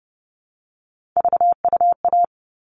Morse code, heard as a keyed tone, about 30 wpm, 700 Hz. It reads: VUA
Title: All Cut Numbers - 3-Digit 30wpm